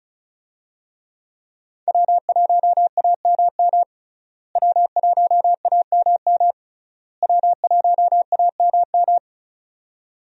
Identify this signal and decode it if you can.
Morse code, heard as a keyed tone, about 36 wpm, 700 Hz. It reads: W1AMM W1AMM W1AMM
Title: USA Call Signs - 5 Characters - Repeated 3x - 35wpm